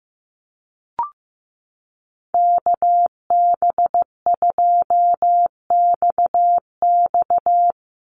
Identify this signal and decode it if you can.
Morse code, heard as a keyed tone, about 15 wpm, 700 Hz. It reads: KB2XX
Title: NA SOTA Chaser Callsigns 15wpm